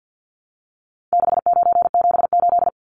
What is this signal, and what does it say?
Morse code, heard as a keyed tone, about 51 wpm, 700 Hz. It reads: 6978
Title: Numbers - 4-Digit 50wpm